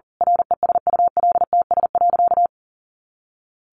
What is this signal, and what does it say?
Morse code, heard as a keyed tone, about 40 wpm, 700 Hz. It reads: RESULTS.
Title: A Sound Of Thunder 40wpm